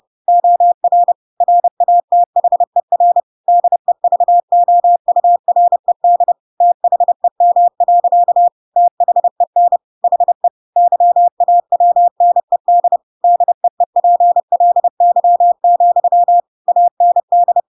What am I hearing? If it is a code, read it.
Morse code, heard as a keyed tone, about 30 wpm, 700 Hz. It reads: OR RATHER DEVOURED THEM. THEN HE YAWNED DEEPLY, AND
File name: Pinocchio / Pinocchio ch7 30 30 ik1hge